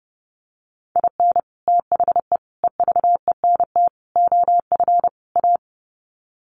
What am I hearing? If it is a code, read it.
Morse code, heard as a keyed tone, about 30 wpm, 700 Hz. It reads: IN THE EVENT OF A
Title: Sets of 5 Words Encore 30wpm